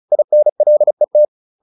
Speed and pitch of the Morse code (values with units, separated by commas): 35 wpm, 600 Hz